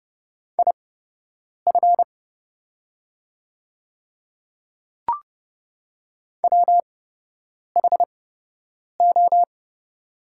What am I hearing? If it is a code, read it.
Morse code, heard as a keyed tone, about 30 wpm, spaced as for 8 wpm, 700 Hz. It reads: IF WHO
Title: 8x Character Spacing - Top 300 Words - 30wpm